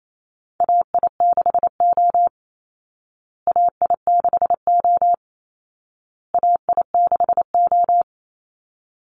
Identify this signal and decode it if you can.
Morse code, heard as a keyed tone, about 28 wpm, 700 Hz. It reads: AI6O AI6O AI6O
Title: Call Signs Repeated 2x - 28wpm